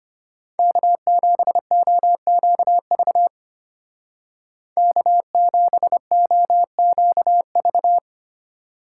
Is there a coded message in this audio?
Yes: Morse code from 0.6 to 8.0 s, heard as a keyed tone, about 32 wpm, 700 Hz. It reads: K7OQV K7OQV